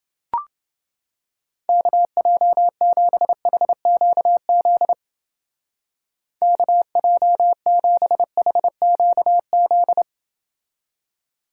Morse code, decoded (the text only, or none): KJ7HQZ KJ7HQZ